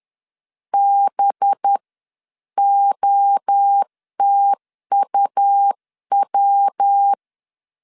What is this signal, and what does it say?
Morse code, heard as a keyed tone, about 10 wpm, spaced as for 9 wpm, 800 Hz. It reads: BOTUW
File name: CW / DET / 1 6 1500C S C 10wpm 4252